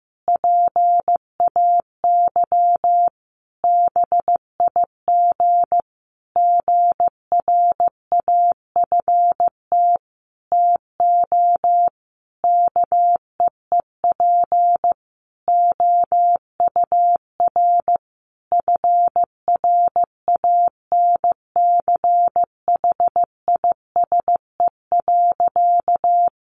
Morse code, heard as a keyed tone, about 15 wpm, 700 Hz. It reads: PAY BIG GRAFT TO KEEP OUR FRANCHISE.
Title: A Sound Of Thunder 15wpm